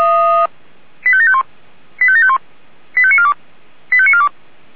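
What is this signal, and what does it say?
Five-tone selective call (ZVEI), heard as quick runs of five tones, tones of about 70 ms. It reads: 85751 85751 85942 85942
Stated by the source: ZVEI five-tone selective call 85729 received by SDR.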